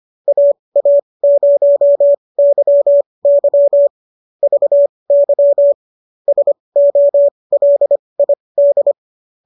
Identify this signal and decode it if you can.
Morse code, heard as a keyed tone, about 25 wpm, spaced as for 20 wpm, 590 Hz. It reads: AA0YY VY SOLID